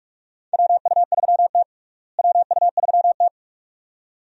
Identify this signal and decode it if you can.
Morse code, heard as a keyed tone, about 46 wpm, 700 Hz. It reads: WU3T WU3T
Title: USA Call Signs - 4 Characters - Repeated 2x - 45wpm